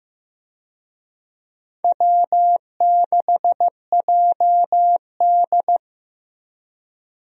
Morse code, heard as a keyed tone, about 15 wpm, 700 Hz. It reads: W6JD
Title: USA Call Signs - 4 Characters - Repeated 3x - 15wpm